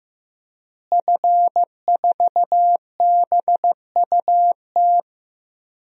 Morse code, heard as a keyed tone, about 15 wpm, 700 Hz. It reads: F4BUT
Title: DXCC-100 Call Signs - 5 Characters - Repeated 3x - 15wpm